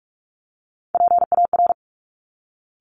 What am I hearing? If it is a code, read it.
Morse code, heard as a keyed tone, about 46 wpm, 700 Hz. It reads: PAR
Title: Top Three Letter Combinations in Words 45wpm